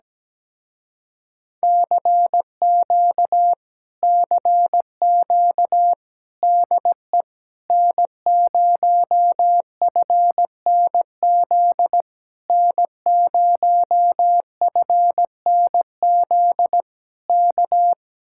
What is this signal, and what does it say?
Morse code, heard as a keyed tone, about 17 wpm, 700 Hz. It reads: CQ CQ DE N0FNZ N0FNZ K
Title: Speed-Racing Common QSO Phrases Encore - 17wpm